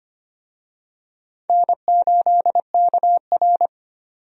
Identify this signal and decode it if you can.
Morse code, heard as a keyed tone, about 25 wpm, 700 Hz. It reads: N8KR
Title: USA Call Signs - 4 Characters - 25wpm